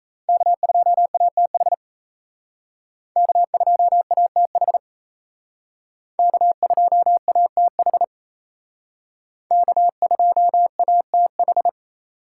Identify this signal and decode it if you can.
Morse code, heard as a keyed tone, about 35 wpm, 700 Hz. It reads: K2ATH K2ATH K2ATH K2ATH
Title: Speed Racing - USA 1x3 Call Signs 28wpm